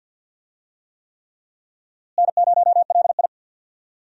Morse code, heard as a keyed tone, about 47 wpm, 700 Hz. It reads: N0LI
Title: USA Call Signs - 4 Characters - Repeated 3x - 50wpm